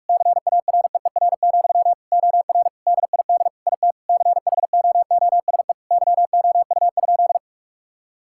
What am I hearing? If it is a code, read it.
Morse code, heard as a keyed tone, about 46 wpm, 700 Hz. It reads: CAREER, OR DID IT CHOOSE YOU?
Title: Sentences from Top 1000 Words - Review Part 6 - 45wpm